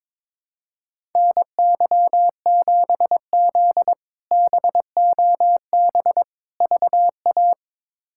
Morse code, heard as a keyed tone, about 22 wpm, 700 Hz. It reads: NY7Z BOB VA
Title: K1USN SST QSO Practice - 22wpm